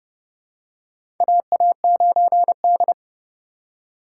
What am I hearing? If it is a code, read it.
Morse code, heard as a keyed tone, about 30 wpm, 700 Hz. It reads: AA9D
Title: USA Call Signs - 2x Character Spacing - 30wpm